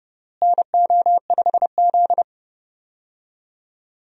Morse code, heard as a keyed tone, about 31 wpm, 700 Hz. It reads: NO5Z
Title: USA Call Signs - 4 Characters - Repeated 3x - 30wpm